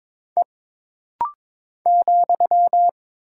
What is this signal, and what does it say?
Morse code, heard as a keyed tone, about 22 wpm, 700 Hz. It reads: E ,
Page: This rapid-fire timing is geared to push your ICR (Instant Character Recognition) skills.